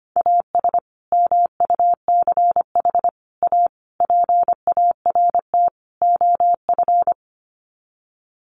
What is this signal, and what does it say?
Morse code, heard as a keyed tone, about 25 wpm, 700 Hz. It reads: AS MUCH A PART OF
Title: Sets of 5 Words Encore2 25wpm